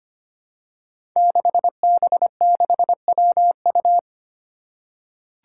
Morse code, heard as a keyed tone, about 25 wpm, 700 Hz. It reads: 6B6WU
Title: Five Challenging Letter-Number Combinations 25wpm